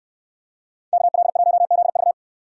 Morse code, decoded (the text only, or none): KL1ZV